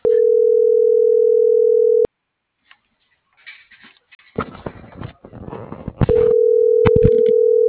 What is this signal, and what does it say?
Ringback tone (North America), 440 + 480 Hz.